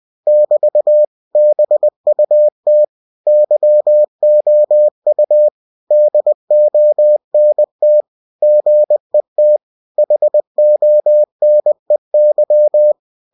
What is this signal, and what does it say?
Morse code, heard as a keyed tone, about 20 wpm, 600 Hz. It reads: <BT> BUT YOU DONT GET HONEY
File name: cw-books / winnie-the-pooh / winnie-20wpm-page-0005